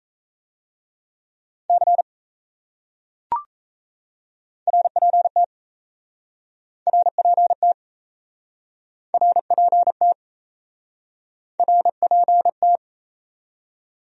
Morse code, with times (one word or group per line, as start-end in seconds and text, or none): 1.7-2.0 C
4.7-5.4 RPT
6.9-7.7 RPT
9.1-10.1 RPT
11.6-12.8 RPT